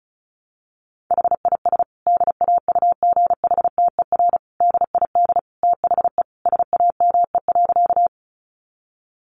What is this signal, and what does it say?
Morse code, heard as a keyed tone, about 34 wpm, 700 Hz. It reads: HIS DAUGHTER DID THE SAME.
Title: Sentences from Top 1500 Words 35wpm